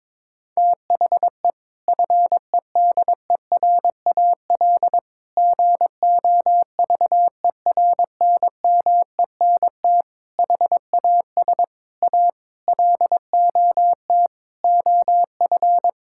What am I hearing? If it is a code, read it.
Morse code, heard as a keyed tone, about 22 wpm, 700 Hz. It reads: THE FEDERAL GOVERNMENT HAS A LOT OF
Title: Sentences from Top 1000 Words - Review Part 5 - 22wpm